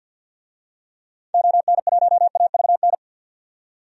Morse code, heard as a keyed tone, about 50 wpm, 700 Hz. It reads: ON1AVN
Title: DXCC-100 Call Signs - 6 Characters - 50wpm